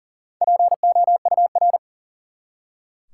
Morse code, heard as a keyed tone, about 40 wpm, 700 Hz. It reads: POUR
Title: Four Letter Words - 40wpm